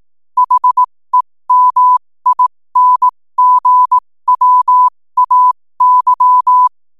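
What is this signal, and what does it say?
Morse code, heard as a keyed tone, about 18 wpm, spaced as for 15 wpm, 1000 Hz. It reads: HEMINGWAY
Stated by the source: 15 wpm Morse code from RNZN Communicators Association.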